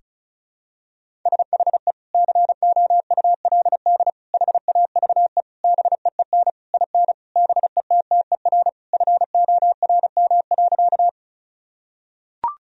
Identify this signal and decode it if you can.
Morse code, heard as a keyed tone, about 34 wpm, 700 Hz. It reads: SHE COULD HAVE BEEN IN BETTER FORM.
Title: Sentences from Top 1000 Words - Review Part 1 - 35wpm